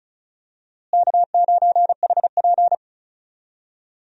Morse code, heard as a keyed tone, about 34 wpm, 700 Hz. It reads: K9HP
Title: USA Call Signs - 4 Characters - Repeated 2x - 35wpm